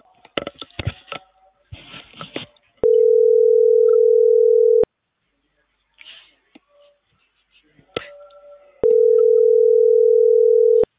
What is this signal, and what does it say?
Ringback tone (North America), 440 + 480 Hz.